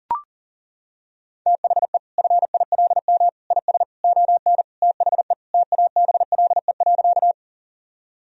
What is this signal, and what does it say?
Morse code, heard as a keyed tone, about 40 wpm, 700 Hz. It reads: THE FILM IS ON THE TABLE.
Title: Sentences from Top 1000 Words - Review Part 3 - 40wpm